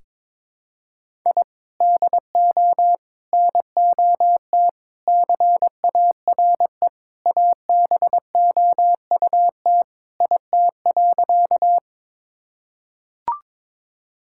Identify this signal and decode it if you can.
Morse code, heard as a keyed tone, about 22 wpm, 700 Hz. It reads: I DO NOT CARE ABOUT IT.
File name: practice / 22wpm / Sentences with 4-Word Phrases - 22wpm